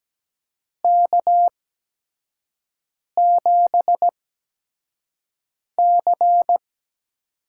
Morse code, heard as a keyed tone, about 17 wpm, spaced as for 5 wpm, 700 Hz. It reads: K7C
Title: USA Call Signs - 4 Characters - 8x Character Spacing - 17wpm